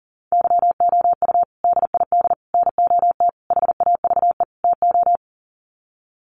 Morse code, heard as a keyed tone, about 40 wpm, 700 Hz. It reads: YOU DID NOT HAVE TO
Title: Sets of 5 Words Encore 40wpm